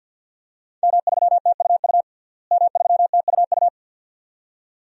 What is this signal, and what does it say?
Morse code, heard as a keyed tone, about 51 wpm, 700 Hz. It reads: M3TUU M3TUU
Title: DXCC-100 Call Signs - 5 Characters - Repeated 2x - 50wpm